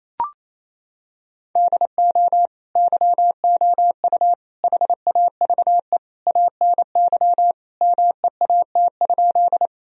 Morse code, heard as a keyed tone, about 28 wpm, 700 Hz. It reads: DO YOU HAVE ANY MEAT?
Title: Sentences from Top 1000 Words - Review Part 3 - 28wpm